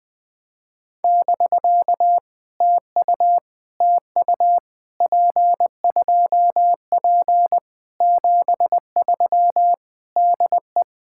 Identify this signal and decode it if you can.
Morse code, heard as a keyed tone, about 20 wpm, 700 Hz. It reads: <BK> TU TU P2P 73 DE
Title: POTA QSO Practice - 20wpm